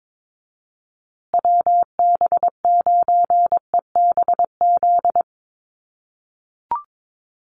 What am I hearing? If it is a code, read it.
Morse code, heard as a keyed tone, about 22 wpm, 700 Hz. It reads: WB9EBZ
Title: USA Call Signs Repeated 3x 22wpm